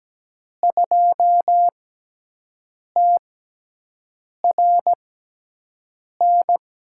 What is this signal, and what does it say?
Morse code, heard as a keyed tone, about 17 wpm, spaced as for 6 wpm, 700 Hz. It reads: 2TRN
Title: USA Call Signs - 6x Character Spacing - 17wpm